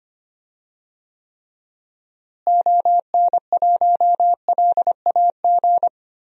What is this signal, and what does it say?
Morse code, heard as a keyed tone, about 25 wpm, 700 Hz. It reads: ON1LAG
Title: DXCC-100 Call Signs - 6 Characters - Repeated 3x - 25wpm